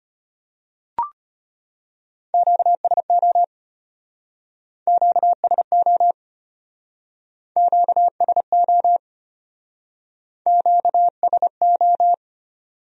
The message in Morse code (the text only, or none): QSO QSO QSO QSO